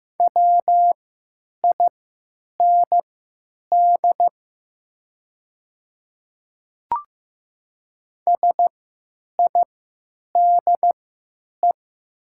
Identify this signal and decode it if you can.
Morse code, heard as a keyed tone, about 15 wpm, spaced as for 9 wpm, 700 Hz. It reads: WIND SIDE
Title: 3x Character Spacing - Top 500 Words - 15wpm